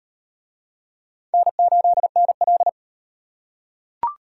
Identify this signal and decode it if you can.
Morse code, heard as a keyed tone, about 37 wpm, 700 Hz. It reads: N8NL